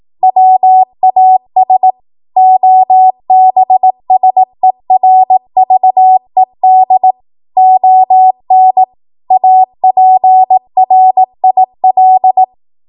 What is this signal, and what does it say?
Morse code, heard as a keyed tone, about 18 wpm, 750 Hz. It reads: WAS OBSERVED ON APRIL
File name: Morse / 18WPM